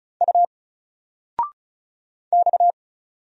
Morse code, heard as a keyed tone, about 35 wpm, 700 Hz. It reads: U X
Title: Single Letter-Number 35wpm